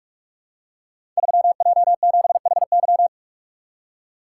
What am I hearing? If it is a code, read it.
Morse code, heard as a keyed tone, about 44 wpm, 700 Hz. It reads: 3J7HY
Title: Code Groups - 5 Letter:Numbers - 45wpm